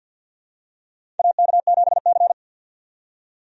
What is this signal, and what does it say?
Morse code, heard as a keyed tone, about 51 wpm, 700 Hz. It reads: AK7C
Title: USA Call Signs - 4 Characters - 50wpm